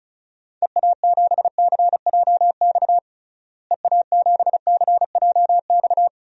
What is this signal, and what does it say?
Morse code, heard as a keyed tone, about 34 wpm, 700 Hz. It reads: EA7CJX EA7CJX
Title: DXCC-100 Call Signs - 6 Characters - Repeated 3x - 35wpm